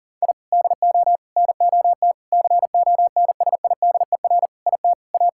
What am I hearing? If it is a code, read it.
Morse code, heard as a keyed tone, about 37 wpm, 700 Hz. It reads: I DO NOT CONSIDER IT A
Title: Sentences from Top 1000 Words - Review Part 6 - 40wpm